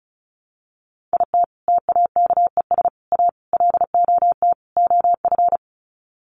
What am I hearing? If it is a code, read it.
Morse code, heard as a keyed tone, about 34 wpm, 700 Hz. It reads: IT TAKES A LOT OF
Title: Sets of 5 Words Encore 35wpm